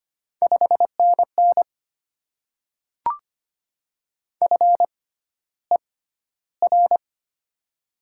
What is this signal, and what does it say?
Morse code, heard as a keyed tone, about 25 wpm, spaced as for 13 wpm, 700 Hz. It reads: 5NN FER
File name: Top 20 QSO Elements - 6x Character Spacing - 25wpm